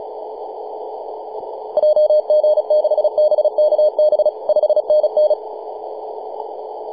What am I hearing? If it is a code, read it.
Morse code, heard as a keyed tone, about 37 wpm, 610 Hz. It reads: JG6BKB 5NN
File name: s79hn-24-910mhz-cw